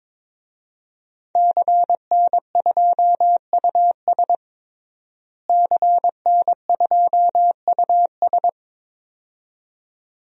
Morse code, heard as a keyed tone, about 22 wpm, 700 Hz. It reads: CN2US CN2US
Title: DXCC-100 Call Signs - 5 Characters - Repeated 2x - 22wpm